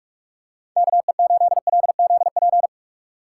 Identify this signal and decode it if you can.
Morse code, heard as a keyed tone, about 46 wpm, 700 Hz. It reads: KE8LZP
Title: USA Call Signs - 6 Characters - 6x Character Spacing - 45wpm